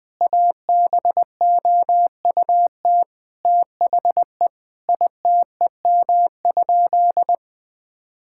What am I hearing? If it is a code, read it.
Morse code, heard as a keyed tone, about 21 wpm, 700 Hz. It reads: ABOUT THE ITEM?
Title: Sentences from Top 1000 Words - Review Part 6 - 20wpm